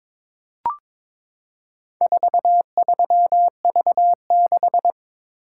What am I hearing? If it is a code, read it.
Morse code, heard as a keyed tone, about 22 wpm, 700 Hz. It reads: 43V6
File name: practice / 22wpm / Four Letter-Number Combinations - 22wpm